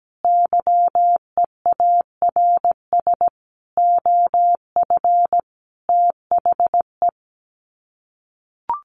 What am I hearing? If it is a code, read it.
Morse code, heard as a keyed tone, about 17 wpm, 700 Hz. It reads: YEARS OF THE
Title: Sets of 5 Words Encore 17wpm